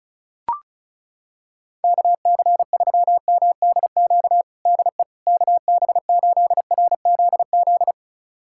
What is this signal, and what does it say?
Morse code, heard as a keyed tone, about 36 wpm, 700 Hz. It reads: KC3MDQ DE KB8RZZ
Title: USA Callsign DE Callsign 35wpm